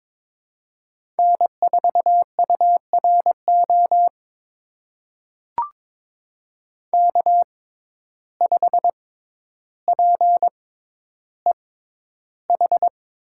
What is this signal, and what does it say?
Morse code, heard as a keyed tone, about 22 wpm, spaced as for 11 wpm, 700 Hz. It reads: N4URO K5PEH
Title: USA Call Signs - 5 Characters - 6x Character Spacing - 22wpm